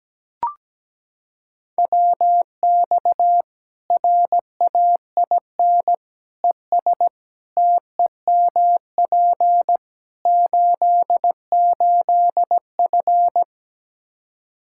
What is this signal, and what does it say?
Morse code, heard as a keyed tone, about 17 wpm, 700 Hz. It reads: WX RAIN ES TEMP 88F
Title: Practice QSO - Exchange 2 - Info - 17wpm